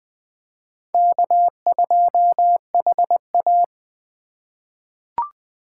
Morse code, heard as a keyed tone, about 20 wpm, 700 Hz. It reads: K2HA
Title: 2023 POTA Hunter Call Signs - 20wpm